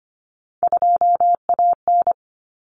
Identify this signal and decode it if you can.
Morse code, heard as a keyed tone, about 25 wpm, 700 Hz. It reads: 2AN
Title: Common Cut Numbers - 3-Digit 25wpm